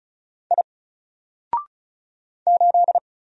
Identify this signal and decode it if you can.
Morse code, heard as a keyed tone, about 36 wpm, 700 Hz. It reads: I 8
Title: Single Letter-Number Alt 35wpm